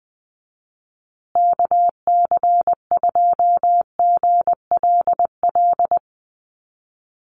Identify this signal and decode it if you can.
Morse code, heard as a keyed tone, about 20 wpm, 700 Hz. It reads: KC2GLL
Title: USA Call Signs Repeated 3x 20wpm